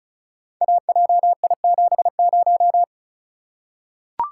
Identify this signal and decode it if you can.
Morse code, heard as a keyed tone, about 36 wpm, 700 Hz. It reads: AJI70
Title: Code Groups - 5 Letter:Numbers - 35wpm